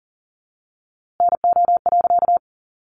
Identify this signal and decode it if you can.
Morse code, heard as a keyed tone, about 40 wpm, 700 Hz. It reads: NO.
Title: A Sound Of Thunder 40wpm